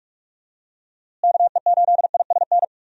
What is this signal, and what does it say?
Morse code, heard as a keyed tone, about 41 wpm, 700 Hz. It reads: KE8ISN
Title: USA Call Signs - 6 Characters - Repeated 2x - 45wpm